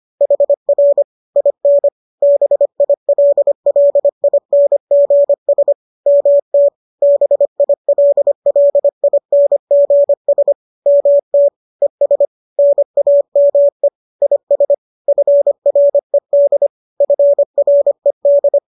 Morse code, heard as a keyed tone, about 25 wpm, 570 Hz. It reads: HR IN BILLINGS MT BILLINGS MT ES NAME IS FRED FRED